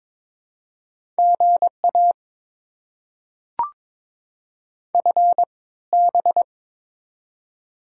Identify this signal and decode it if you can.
Morse code, heard as a keyed tone, about 22 wpm, spaced as for 16 wpm, 700 Hz. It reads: GA FB
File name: Top 90 QSO Elements - 3x Character Spacing - 22wpm